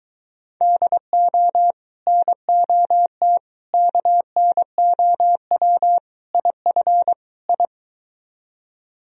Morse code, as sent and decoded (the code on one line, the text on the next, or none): -.. --- / -. --- - / -.- -. --- .-- / .. ..-. / ..
DO NOT KNOW IF I